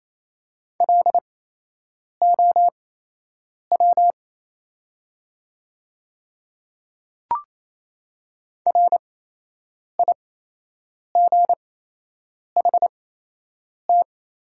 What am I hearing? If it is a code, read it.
Morse code, heard as a keyed tone, about 28 wpm, spaced as for 8 wpm, 700 Hz. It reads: LOW RIGHT
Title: 8x Character Spacing - Top 200 Words - 28wpm